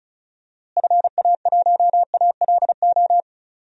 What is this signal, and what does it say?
Morse code, heard as a keyed tone, about 36 wpm, 700 Hz. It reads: FA1ALO
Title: DXCC-100 Call Signs - 6 Characters - Repeated 2x - 35wpm